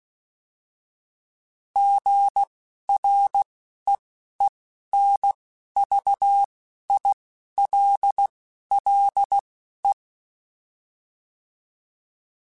Morse code, heard as a keyed tone, about 16 wpm, spaced as for 11 wpm, 800 Hz. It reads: GREENVILLE